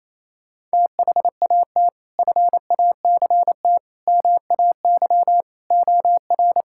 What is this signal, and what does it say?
Morse code, heard as a keyed tone, about 27 wpm, 700 Hz. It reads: THAT FACT MAY OR
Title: Sentences from Top 1000 Words - Review Part 6 - 28wpm